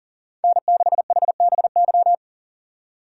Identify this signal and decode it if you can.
Morse code, heard as a keyed tone, about 40 wpm, 700 Hz. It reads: N6HBY
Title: USA Call Signs - 5 Characters - 3x Character Spacing - 40wpm